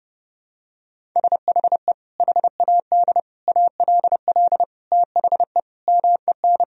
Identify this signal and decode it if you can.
Morse code, heard as a keyed tone, about 30 wpm, 700 Hz. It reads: SHE HAD ALL THE MEN
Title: Sentences from Top 1000 Words - Review Part 1 - 30wpm